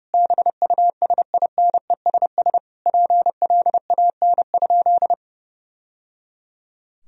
Morse code, heard as a keyed tone, about 32 wpm, 700 Hz. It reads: BUSINESS PLAN?
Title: Sentences from Top 1000 Words - Review Part 3 - 30wpm